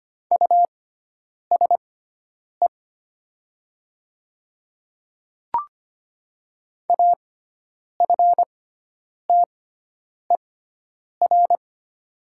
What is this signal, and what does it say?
Morse code, heard as a keyed tone, about 25 wpm, spaced as for 9 wpm, 700 Hz. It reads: USE AFTER
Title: 6x Character Spacing - Top 300 Words - 25wpm